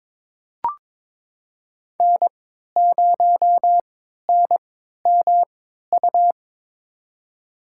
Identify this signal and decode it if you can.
Morse code, heard as a keyed tone, about 22 wpm, spaced as for 13 wpm, 700 Hz. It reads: N0NMU
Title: USA Call Signs - 5 Characters - 3x Character Spacing - 22wpm